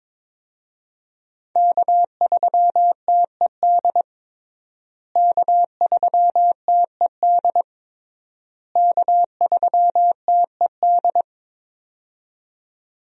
Morse code, heard as a keyed tone, about 22 wpm, 700 Hz. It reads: K3TED K3TED K3TED
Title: USA Call Signs - 5 Characters - Repeated 3x - 22wpm